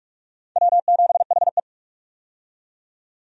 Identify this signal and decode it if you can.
Morse code, heard as a keyed tone, about 45 wpm, 700 Hz. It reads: W7HE
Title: USA Call Signs - 4 Characters - 45wpm